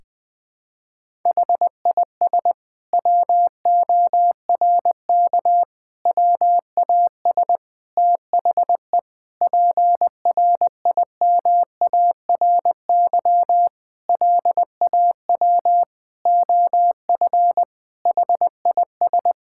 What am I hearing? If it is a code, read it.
Morse code, heard as a keyed tone, about 21 wpm, 700 Hz. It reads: HIS WORK WAS THE PRIMARY LAW OF HIS
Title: Sentences from Top 1000 Words - Review Part 5 - 20wpm